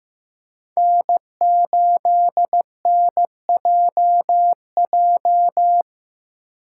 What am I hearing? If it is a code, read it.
Morse code, heard as a keyed tone, about 18 wpm, 700 Hz. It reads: N8NJJ
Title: Speed Racing - USA 1x3 Call Signs 15wpm